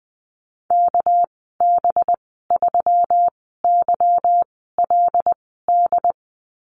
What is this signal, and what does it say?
Morse code, heard as a keyed tone, about 20 wpm, spaced as for 14 wpm, 700 Hz. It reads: KB3YLD
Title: 200 Lesson 32g - Letter k - Call Signs 20wpm